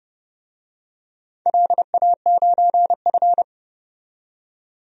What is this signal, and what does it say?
Morse code, heard as a keyed tone, about 30 wpm, 700 Hz. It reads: LA9F
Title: DXCC-100 Call Signs - 4 Characters - 30wpm